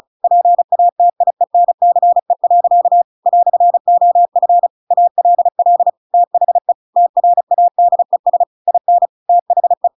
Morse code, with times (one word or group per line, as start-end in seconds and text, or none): 0.2-3.0 PATIENCE.
3.3-4.7 "OF
4.9-5.9 ALL
6.1-6.7 THE
7.0-8.4 TRADES
8.7-9.1 IN
9.3-9.9 THE